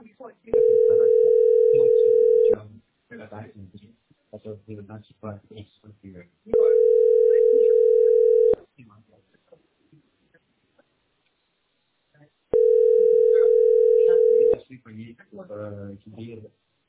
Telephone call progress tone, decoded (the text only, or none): ringback tone